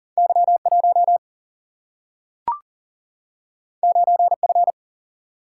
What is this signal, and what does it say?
Morse code, heard as a keyed tone, about 39 wpm, 700 Hz. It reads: Y1 9F
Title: Single-Letter Single-Number 40wpm